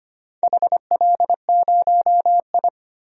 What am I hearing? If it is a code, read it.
Morse code, heard as a keyed tone, about 25 wpm, 700 Hz. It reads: HL0I
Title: DXCC-100 Call Signs - 4 Characters - 25wpm